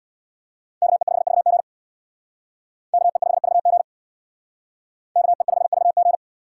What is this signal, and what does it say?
Morse code, heard as a keyed tone, about 62 wpm, 700 Hz. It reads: KE5VG KE5VG KE5VG
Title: Speed Racing - USA 2x2 Call Signs 50wpm